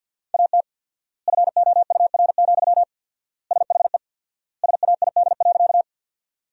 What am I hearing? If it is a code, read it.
Morse code, heard as a keyed tone, about 53 wpm, 700 Hz. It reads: AT FOUR, SHE SAID.
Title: 4x Word-Spacing - Sentences from Top 300 Words - 50wpm